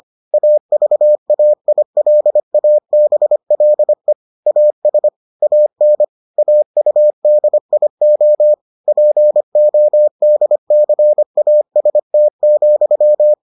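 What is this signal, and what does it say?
Morse code, heard as a keyed tone, about 25 wpm, 600 Hz. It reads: AVAILABLE AS AN AUDIO PODCAST,
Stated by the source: Morse edition of the Foundations of Amateur Radio episodes.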